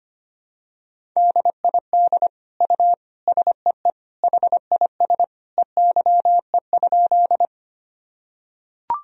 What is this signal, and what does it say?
Morse code, heard as a keyed tone, about 25 wpm, 700 Hz. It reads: DID U SEE HIS EYE?
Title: Sentences from Top 300 Words and Abbreviations 25wpm